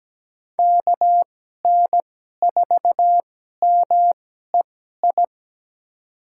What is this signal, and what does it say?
Morse code, heard as a keyed tone, about 17 wpm, spaced as for 12 wpm, 700 Hz. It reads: KN4MEI
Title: USA Call Signs - 6 Characters - 2x Character Spacing - 17wpm